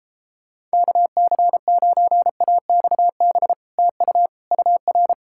Morse code, heard as a keyed tone, about 31 wpm, 700 Hz. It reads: KC9AXB TU UR